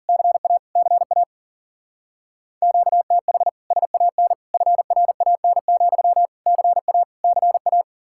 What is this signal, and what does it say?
Morse code, heard as a keyed tone, about 49 wpm, 700 Hz. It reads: CA CA QTH SAN FRAN, CA CA
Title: Speed-Racing Common QSO Phrases Encore - 40wpm